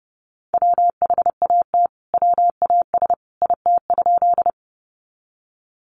Morse code, heard as a keyed tone, about 30 wpm, 700 Hz. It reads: WHAT WAS IT?
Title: Sentences from Top 100 Words 30wpm